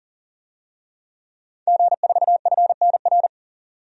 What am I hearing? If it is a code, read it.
Morse code, heard as a keyed tone, about 40 wpm, 700 Hz. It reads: G4FNR
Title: DXCC-100 Call Signs - 5 Characters - 40wpm